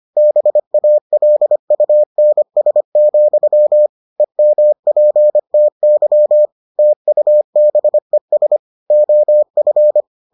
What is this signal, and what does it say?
Morse code, heard as a keyed tone, about 25 wpm, 600 Hz. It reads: BALUNS, EMPTY TUBES OF